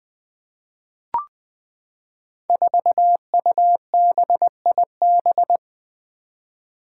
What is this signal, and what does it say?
Morse code, heard as a keyed tone, about 20 wpm, 700 Hz. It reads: 4UBIB
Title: Five Challenging Letter-Number Combinations 20wpm